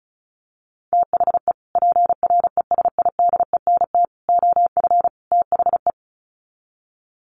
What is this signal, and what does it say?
Morse code, heard as a keyed tone, about 34 wpm, 700 Hz. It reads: THE PRESIDENT OF THE
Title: Sets of 4 Words 35wpm